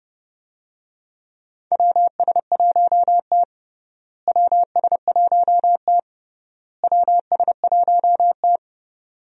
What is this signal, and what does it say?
Morse code, heard as a keyed tone, about 29 wpm, 700 Hz. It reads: WS1T WS1T WS1T
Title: USA Call Signs - 4 Characters - Repeated 3x - 30wpm